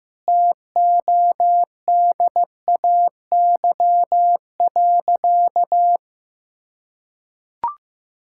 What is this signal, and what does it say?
Morse code, heard as a keyed tone, about 15 wpm, 700 Hz. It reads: TODAY.
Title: Sentences from Top 1600 Words 15wpm